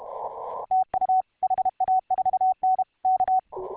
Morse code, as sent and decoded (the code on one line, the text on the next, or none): - ..- / .... .- ....- -. / -.-
TU HA4N K